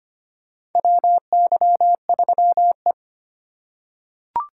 Morse code, heard as a keyed tone, about 25 wpm, 700 Hz. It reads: WY3E
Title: USA Call Signs - 4 Characters - 3x Character Spacing - 25wpm